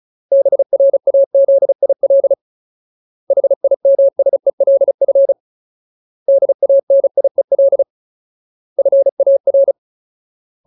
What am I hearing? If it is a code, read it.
Morse code, heard as a keyed tone, about 34 wpm, 550 Hz. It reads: BRAZIL HIMSELF DANIEL FAR
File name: Ultimate Word List / 35WPM Ultimate